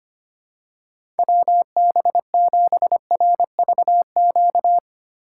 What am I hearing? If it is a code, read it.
Morse code, heard as a keyed tone, about 25 wpm, 700 Hz. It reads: WB7RVQ
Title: USA Call Signs - 6 Characters - 3x Character Spacing - 25wpm